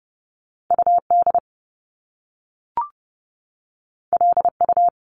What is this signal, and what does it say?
Morse code, heard as a keyed tone, about 30 wpm, 700 Hz. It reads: UD LU
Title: Top Two Letter Combinations in Words 30wpm